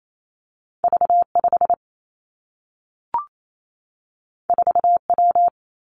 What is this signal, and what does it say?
Morse code, heard as a keyed tone, about 28 wpm, 700 Hz. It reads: V 5 4 W
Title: Two Challenging Letter-Number Combinations 28wpm